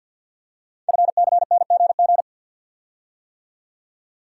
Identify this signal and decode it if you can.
Morse code, heard as a keyed tone, about 48 wpm, 700 Hz. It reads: FCNGG
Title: Code Groups - 5 Letters - 50wpm